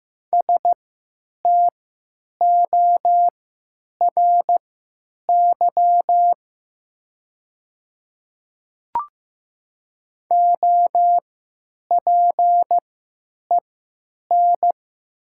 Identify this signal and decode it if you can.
Morse code, heard as a keyed tone, about 15 wpm, spaced as for 9 wpm, 700 Hz. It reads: STORY OPEN
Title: 3x Character Spacing - Top 300 Words - 15wpm